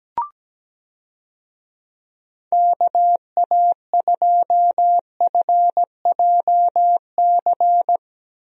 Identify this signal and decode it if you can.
Morse code, heard as a keyed tone, about 17 wpm, 700 Hz. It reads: KA2FJC
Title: USA Call Signs - 6 Characters - Repeated 2x - 17wpm